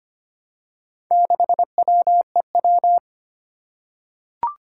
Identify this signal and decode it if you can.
Morse code, heard as a keyed tone, about 25 wpm, 700 Hz. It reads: 6WEW
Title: Four Challenging Letter-Number Combinations 25wpm